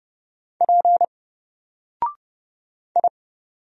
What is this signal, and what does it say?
Morse code, heard as a keyed tone, about 30 wpm, 700 Hz. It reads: P I
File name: Single-Letter-Number-30wpm